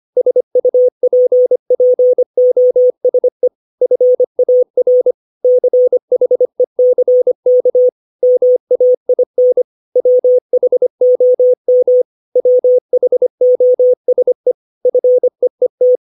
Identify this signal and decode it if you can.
Morse code, heard as a keyed tone, about 25 wpm, 500 Hz. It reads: SUPPOSE FAR CHECK MAIN WHOM WHOSE FEET
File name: 632 words / 632 Words 25WPM